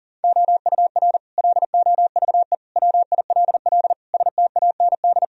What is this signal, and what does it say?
Morse code, heard as a keyed tone, about 40 wpm, 700 Hz. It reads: OUR LOVE WILL STAND
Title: Sentences from Top 1000 Words - Review Part 3 - 40wpm